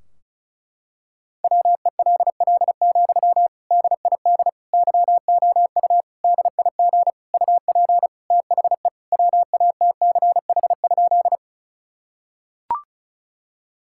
Morse code, heard as a keyed tone, about 36 wpm, 700 Hz. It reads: WELL, DID YOU DIG UP THE WATCH?
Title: Sentences from Top 1600 Words 35wpm